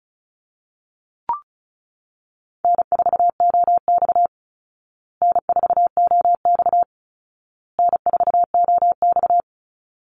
Morse code, heard as a keyed tone, about 34 wpm, 700 Hz. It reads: N4OX N4OX N4OX
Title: Easy Call Signs Repeated 3x - 35wpm